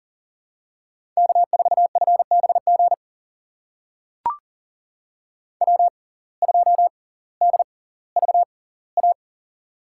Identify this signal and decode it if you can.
Morse code, heard as a keyed tone, about 39 wpm, spaced as for 20 wpm, 700 Hz. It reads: K4FBG W2DVA
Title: USA Call Signs - 5 Characters - 6x Character Spacing - 40wpm